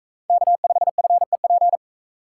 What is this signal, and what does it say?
Morse code, heard as a keyed tone, about 42 wpm, 700 Hz. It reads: K5FEP